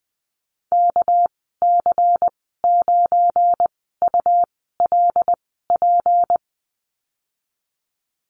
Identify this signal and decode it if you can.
Morse code, heard as a keyed tone, about 20 wpm, spaced as for 14 wpm, 700 Hz. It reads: KC9ULP
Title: 231 Lesson 37g - Letter x - Call Signs 20wpm